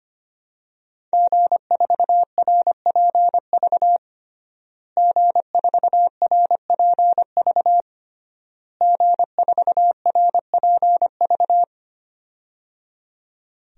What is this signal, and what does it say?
Morse code, heard as a keyed tone, about 25 wpm, 700 Hz. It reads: G4RPV G4RPV G4RPV
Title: DXCC-100 Call Signs - 5 Characters - Repeated 3x - 25wpm